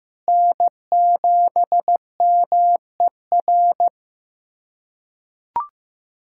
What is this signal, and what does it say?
Morse code, heard as a keyed tone, about 15 wpm, 700 Hz. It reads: N7MER
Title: USA Callsign DE Callsign - 5 Characters - 15wpm